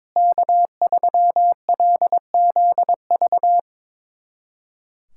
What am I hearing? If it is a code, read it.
Morse code, heard as a keyed tone, about 21 wpm, 700 Hz. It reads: K3LZV
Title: USA Callsign DE Callsign - 5 Characters - 22wpm